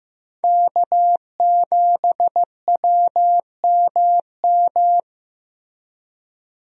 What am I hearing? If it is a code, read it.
Morse code, heard as a keyed tone, about 15 wpm, 700 Hz. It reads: K7WMM
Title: USA Call Signs - 5 Characters - Repeated 3x - 15wpm